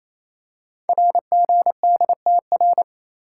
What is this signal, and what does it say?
Morse code, heard as a keyed tone, about 28 wpm, 700 Hz. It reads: RGDTR
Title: Code Groups - 5 Letters - 28wpm